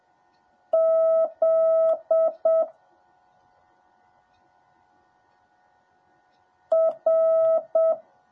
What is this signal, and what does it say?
Morse code, heard as a keyed tone, about 7 wpm, 650 Hz. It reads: Z R
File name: jh8cbh / 2021cw / cw10a